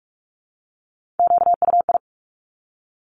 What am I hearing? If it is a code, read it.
Morse code, heard as a keyed tone, about 46 wpm, 700 Hz. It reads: QUI
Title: Top Three Letter Combinations in Words 45wpm